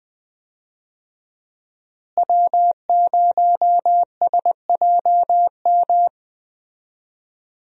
Morse code, heard as a keyed tone, about 20 wpm, 700 Hz. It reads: W0SJM
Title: USA Call Signs - 5 Characters - Repeated 3x - 20wpm